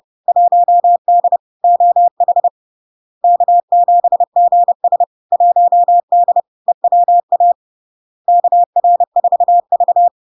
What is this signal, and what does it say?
Morse code, heard as a keyed tone, about 30 wpm, 700 Hz. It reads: 1D OH K7GS 1D EWA KR4V
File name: LARK-FD-2022-Log-in-CW-30wpm-700-Hz